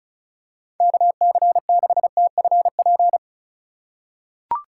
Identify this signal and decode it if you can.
Morse code, heard as a keyed tone, about 36 wpm, 700 Hz. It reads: KC6TFP
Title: USA Call Signs - 6 Characters - 3x Character Spacing - 35wpm